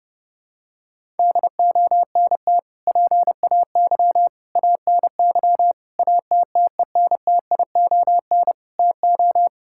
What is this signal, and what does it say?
Morse code, heard as a keyed tone, about 28 wpm, 700 Hz. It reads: DONT PAY ANY ATTENTION TO
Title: Sentences from Top 1000 Words - Review Part 6 - 30wpm